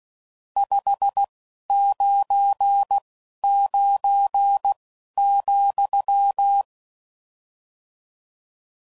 Morse code, heard as a keyed tone, about 16 wpm, spaced as for 11 wpm, 800 Hz. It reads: 599,
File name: CW Code Groups @10WPM / RSTs / Your RST's